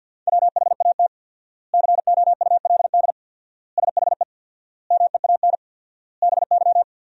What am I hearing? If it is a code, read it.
Morse code, heard as a keyed tone, about 50 wpm, 700 Hz. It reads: WHAT COULD SHE MEAN BY
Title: 4x Word-Spacing - Sentences from Top 200 Words - 50wpm